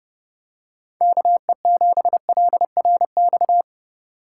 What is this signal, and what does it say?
Morse code, heard as a keyed tone, about 29 wpm, 700 Hz. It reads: KE7LRX